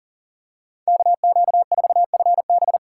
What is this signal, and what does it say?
Morse code, heard as a keyed tone, about 37 wpm, 700 Hz. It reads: KQ4FB
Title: USA Call Signs - 5 Characters - 3x Character Spacing - 40wpm